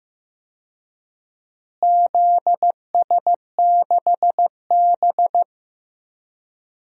Morse code, heard as a keyed tone, about 15 wpm, 700 Hz. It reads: ZS6B
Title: DXCC-100 Call Signs - 4 Characters - Repeated 3x - 15wpm